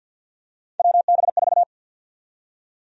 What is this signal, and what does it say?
Morse code, heard as a keyed tone, about 52 wpm, 700 Hz. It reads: WB4
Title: USA Call Sign Prefixes - Repeated 2x - 50wpm